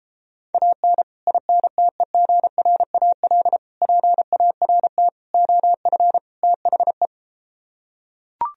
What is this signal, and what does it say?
Morse code, heard as a keyed tone, about 33 wpm, 700 Hz. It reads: AN INTEGRAL PART OF THE